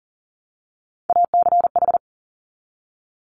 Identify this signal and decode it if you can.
Morse code, heard as a keyed tone, about 38 wpm, 700 Hz. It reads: ACH
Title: Top Three Letter Combinations in Words 40wpm